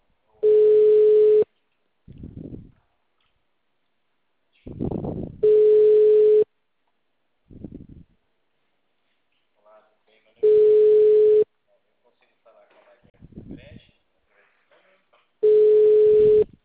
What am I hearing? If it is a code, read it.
Ringback tone (Europe, 425 Hz), with speech.